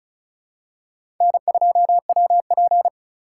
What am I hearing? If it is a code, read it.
Morse code, heard as a keyed tone, about 34 wpm, 700 Hz. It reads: N2WP
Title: USA Call Signs - 3x Character Spacing - 35wpm